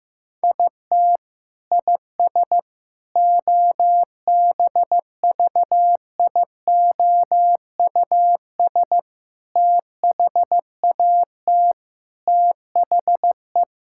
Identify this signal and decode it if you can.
Morse code, heard as a keyed tone, about 15 wpm, 700 Hz. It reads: IT IS OBVIOUS THAT THE
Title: Sentences from Top 1600 Words 15wpm